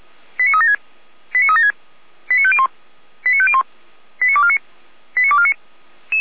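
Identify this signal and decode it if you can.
Five-tone selective call (ZVEI), heard as quick runs of five tones, tones of about 70 ms. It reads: 89376 89376 89501 89501 89249 89249
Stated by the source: ZVEI five-tone call 89337 received with an SDR, archived by Feuerwehr Orlamünde.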